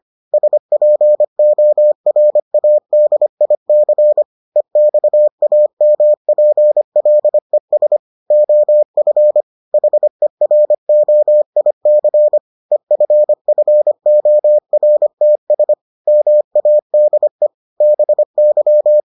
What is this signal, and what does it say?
Morse code, heard as a keyed tone, about 25 wpm, 600 Hz. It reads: SPORADIC EXAMPLES OF HEROIC EFFORTS MADE BY